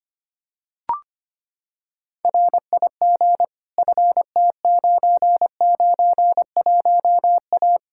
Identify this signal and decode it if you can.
Morse code, heard as a keyed tone, about 25 wpm, 700 Hz. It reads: RIG FT991A
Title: K1USN SST QSO Practice 25wpm